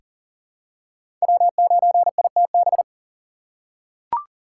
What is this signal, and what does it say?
Morse code, heard as a keyed tone, about 40 wpm, 700 Hz. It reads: W9ITB
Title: USA Call Signs - 5 Characters - 2x Character Spacing - 40wpm